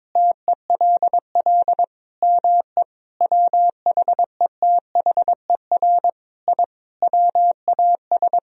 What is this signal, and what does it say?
Morse code, heard as a keyed tone, about 22 wpm, 700 Hz. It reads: TELL ME WHETHER I WAS
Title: Sentences from Top 1000 Words - Review Part 3 - 22wpm